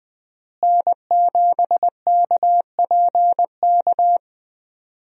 Morse code, heard as a keyed tone, about 20 wpm, 700 Hz. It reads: N7KPK
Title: USA Call Signs - 5 Characters - Repeated 2x - 20wpm